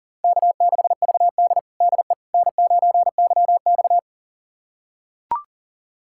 Morse code, heard as a keyed tone, about 37 wpm, 700 Hz. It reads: K6VD DE N9YX
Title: USA Callsign DE Callsign - 4 Characters - 40wpm